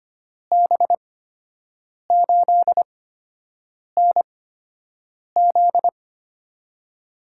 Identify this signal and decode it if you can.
Morse code, heard as a keyed tone, about 25 wpm, spaced as for 7 wpm, 700 Hz. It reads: B8NZ
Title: USA Call Signs - 5 Characters - 8x Character Spacing - 25wpm